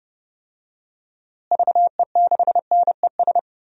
Morse code, heard as a keyed tone, about 31 wpm, 700 Hz. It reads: VE6NES
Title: DXCC-100 Call Signs - 6 Characters - Repeated 3x - 30wpm